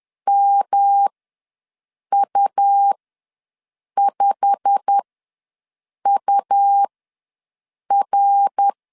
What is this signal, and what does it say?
Morse code, heard as a keyed tone, about 10 wpm, spaced as for 6 wpm, 800 Hz. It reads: MU5UR
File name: LEZIONI / 9 2867 ws 10wpm 4215